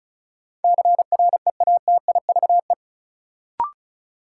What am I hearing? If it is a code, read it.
Morse code, heard as a keyed tone, about 36 wpm, 700 Hz. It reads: CREATIVE
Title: Sentences from Top 1000 Words - Review Part 5 - New Word Drill - 35wpm